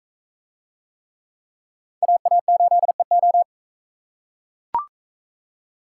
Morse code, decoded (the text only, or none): AA8EO